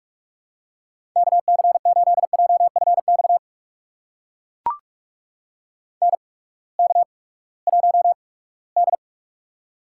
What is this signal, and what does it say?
Morse code, heard as a keyed tone, about 42 wpm, spaced as for 22 wpm, 700 Hz. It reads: KC8JFX NK1D
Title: USA Call Signs - 8x Character Spacing - 45wpm